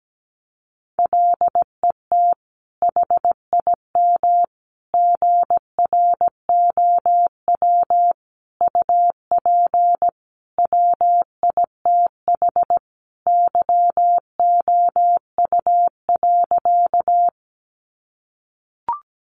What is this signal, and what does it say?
Morse code, heard as a keyed tone, about 17 wpm, 700 Hz. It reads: LET HIM GROW UP WITH YOU.
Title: Sentences from Top 300 Words 17wpm